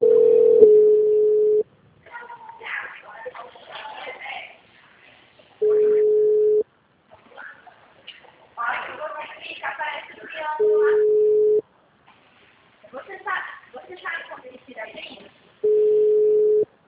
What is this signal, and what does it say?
Ringback tone (Europe, 425 Hz), with speech.